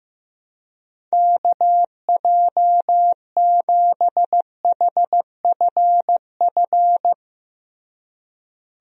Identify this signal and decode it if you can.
Morse code, heard as a keyed tone, about 15 wpm, 700 Hz. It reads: KJ7HFF
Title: USA Call Signs - 6 Characters - Repeated 3x - 15wpm